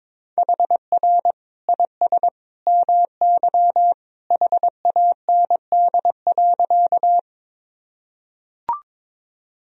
Morse code, heard as a keyed tone, about 22 wpm, 700 Hz. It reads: HR IS MY HAND.
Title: Sentences from Top 200 Words and Abbreviations 22wpm